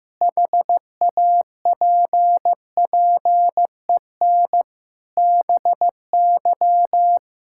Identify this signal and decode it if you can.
Morse code, heard as a keyed tone, about 15 wpm, 700 Hz. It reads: HAPPEN BY
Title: Sentences from Top 1000 Words - Review Part 5 - 15wpm